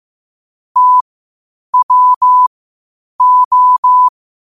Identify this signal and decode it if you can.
Morse code, heard as a keyed tone, about 15 wpm, spaced as for 8 wpm, 1000 Hz. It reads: TWO
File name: qotdfiles / 8wpm